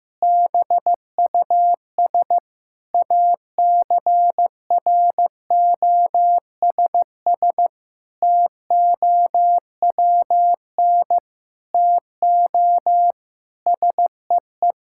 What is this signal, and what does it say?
Morse code, heard as a keyed tone, about 16 wpm, 700 Hz. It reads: BUS ACROSS TOWN TO SEE
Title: Sentences from Top 1000 Words - Review Part 6 - 15wpm